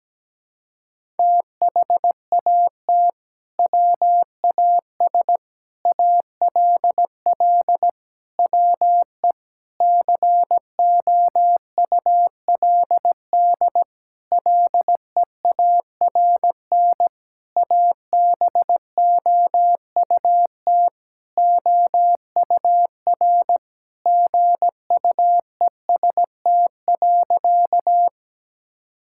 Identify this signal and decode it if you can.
Morse code, heard as a keyed tone, about 17 wpm, 700 Hz. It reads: THAT WAS ALL WE COULD LEARN ABOUT OUR GUEST.
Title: Sentences from Top 1600 Words 17wpm